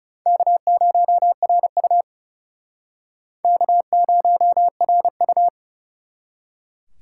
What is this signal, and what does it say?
Morse code, heard as a keyed tone, about 37 wpm, 700 Hz. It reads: K0RU K0RU
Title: Speed Racing - USA 1x2 Call Signs 30wpm